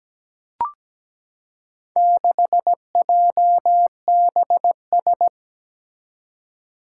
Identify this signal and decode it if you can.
Morse code, heard as a keyed tone, about 17 wpm, 700 Hz. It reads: 6JBS
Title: Four Challenging Letter-Number Combinations 17wpm